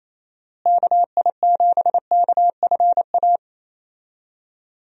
Morse code, heard as a keyed tone, about 28 wpm, 700 Hz. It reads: KI7KFA
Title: USA Call Signs - 6 Characters - Repeated 3x - 28wpm